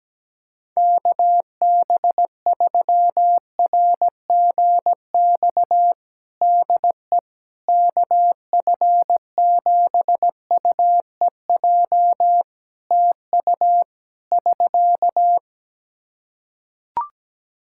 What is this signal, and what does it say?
Morse code, heard as a keyed tone, about 17 wpm, 700 Hz. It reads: KB3RGX DE KF7UEJ TU <SK>
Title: Practice QSO - Exchange 4 - Ending - 17wpm